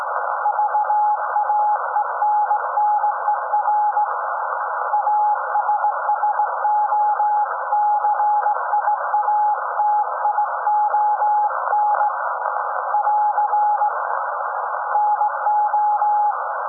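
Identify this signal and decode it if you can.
Morse code, heard as a keyed tone, about 17 wpm, 840 Hz, with noise. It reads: VAENNER ASSOCIATION <AR> NO
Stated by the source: SAQ Grimeton Radio, Alexanderson 200 kW alternator on 17.2 kHz, received in Potsdam with a very strong signal on World Radio Day.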